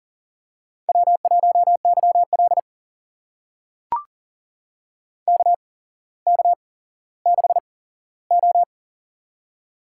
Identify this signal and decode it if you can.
Morse code, heard as a keyed tone, about 38 wpm, spaced as for 17 wpm, 690 Hz. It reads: W1YL KK6O
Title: USA Call Signs - 4 Characters - 8x Character Spacing - 40wpm